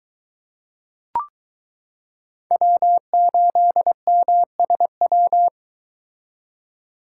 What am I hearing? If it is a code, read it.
Morse code, heard as a keyed tone, about 22 wpm, 700 Hz. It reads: W8MSW